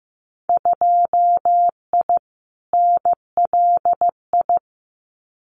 Morse code, heard as a keyed tone, about 15 wpm, 700 Hz. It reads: 2I NLI
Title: Winter Field Day Exchanges 15wpm